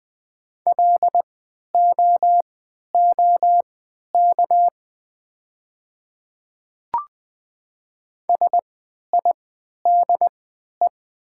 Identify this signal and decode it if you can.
Morse code, heard as a keyed tone, about 20 wpm, spaced as for 11 wpm, 700 Hz. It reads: LOOK SIDE
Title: 3x Character Spacing - Top 200 Words - 20wpm